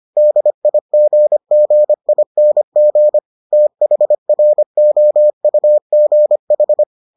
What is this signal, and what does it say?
Morse code, heard as a keyed tone, about 25 wpm, 600 Hz. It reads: DIGGING THROUGH